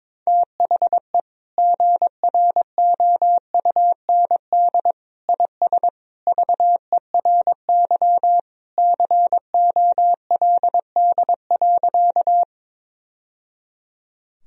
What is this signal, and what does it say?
Morse code, heard as a keyed tone, about 22 wpm, 700 Hz. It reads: THE GROUND IS VERY COLD.
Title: Sentences from Top 1000 Words - Review Part 2 - 22wpm